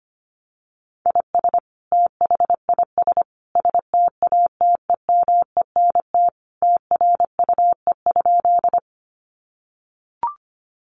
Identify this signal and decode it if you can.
Morse code, heard as a keyed tone, about 25 wpm, 700 Hz. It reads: IS THIS STATEMENT TRUE?
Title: Sentences from Top 1200 Words 25wpm